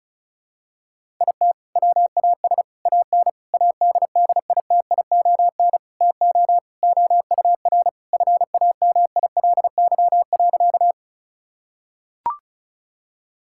Morse code, heard as a keyed tone, about 36 wpm, 700 Hz. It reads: IT WAS AN ADDITION TO OUR FAMILY.
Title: Sentences from Top 1000 Words - Review Part 6 - 35wpm